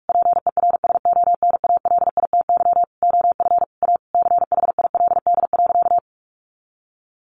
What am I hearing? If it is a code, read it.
Morse code, heard as a keyed tone, about 44 wpm, 700 Hz. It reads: PERSONALITY OF A CHILD.
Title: Sentences from Top 1400 Words 45wpm